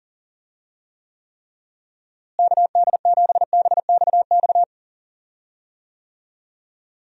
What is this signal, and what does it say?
Morse code, heard as a keyed tone, about 40 wpm, 700 Hz. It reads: KD7BXX
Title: USA Call Signs - 6 Characters - Repeated 3x - 40wpm